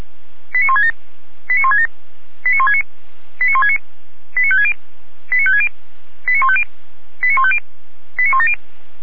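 Five-tone selective call (ZVEI), heard as quick runs of five tones, tones of about 70 ms. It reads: 89167 89167 89169 89169 89560 89560 89150 89150 89170
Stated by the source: ZVEI five-tone call 89220 received by SDR, archived by fire brigade Feuerwehr Orlamünde.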